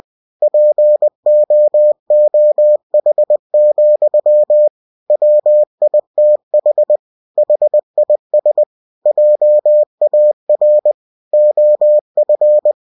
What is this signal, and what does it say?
Morse code, heard as a keyed tone, about 20 wpm, 600 Hz. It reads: POOH, WITH HIS JAR OF
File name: cw-books / winnie-the-pooh / winnie-20wpm-page-0049